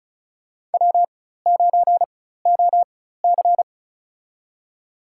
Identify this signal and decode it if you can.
Morse code, heard as a keyed tone, about 36 wpm, spaced as for 16 wpm, 700 Hz. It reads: W9OC
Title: USA Call Signs - 4x Character Spacing - 35wpm